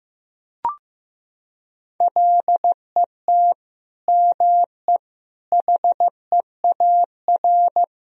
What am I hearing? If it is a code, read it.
Morse code, heard as a keyed tone, about 15 wpm, 700 Hz. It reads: LET ME HEAR
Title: Sentences from Top 1000 Words - Review Part 1 - 15wpm